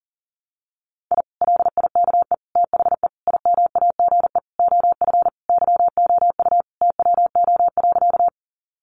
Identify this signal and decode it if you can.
Morse code, heard as a keyed tone, about 40 wpm, 700 Hz. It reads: I LIKE THE IMAGE OF YOU TWO.
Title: Sentences from Top 700 Words 40wpm